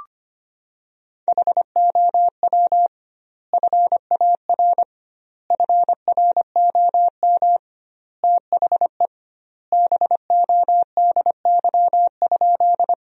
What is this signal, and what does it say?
Morse code, heard as a keyed tone, about 25 wpm, 700 Hz. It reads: HOW FAR FROM THE BODY?
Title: 2x Word-Spacing - Sentences from Top 400 Words - 25wpm